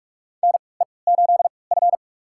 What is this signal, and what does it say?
Morse code, heard as a keyed tone, about 46 wpm, spaced as for 26 wpm, 700 Hz. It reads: NE8F
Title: USA Call Signs - 4 Characters - 3x Character Spacing - 45wpm